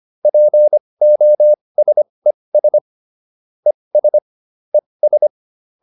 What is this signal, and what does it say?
Morse code, heard as a keyed tone, about 25 wpm, spaced as for 20 wpm, 600 Hz. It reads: POSES ES ES